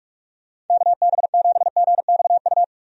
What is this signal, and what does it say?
Morse code, heard as a keyed tone, about 41 wpm, 700 Hz. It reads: KB7GXU
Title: USA Call Signs - 6 Characters - Repeated 3x - 45wpm